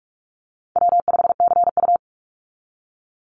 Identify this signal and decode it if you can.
Morse code, heard as a keyed tone, about 44 wpm, 700 Hz. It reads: W5CU
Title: Easy Call Signs Repeated 3x - 45wpm